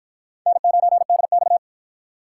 Morse code, heard as a keyed tone, about 48 wpm, 700 Hz. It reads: N9DX